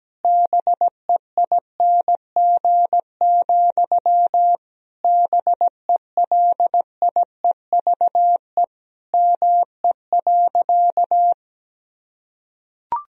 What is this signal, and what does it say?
Morse code, heard as a keyed tone, about 17 wpm, 700 Hz. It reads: BEING, BELIEVE ME.
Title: Sentences from Top 1000 Words - Review Part 3 - 17wpm